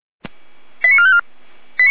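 Five-tone selective call (ZVEI), heard as quick runs of five tones, tones of about 70 ms. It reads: 89454
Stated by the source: ZVEI five-tone selective call 88233, received by SDR.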